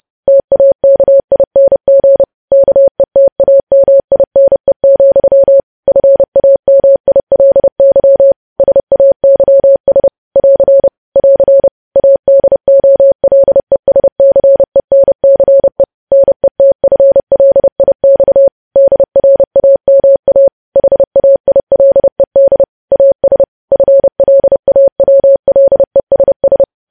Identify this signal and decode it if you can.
Morse code, heard as a keyed tone, about 30 wpm, 550 Hz. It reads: TAKING KETAMINE, FAMILY SAYS <AR> <AR> ADOLESCENCE NETFLIX DRAMA HAILED AS FLAWLESS
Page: news headlines in morse code at 30 wpm